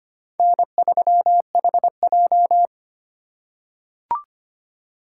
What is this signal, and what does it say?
Morse code, heard as a keyed tone, about 25 wpm, 700 Hz. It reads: N3HJ
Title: USA Call Signs - 4 Characters - 4x Character Spacing - 25wpm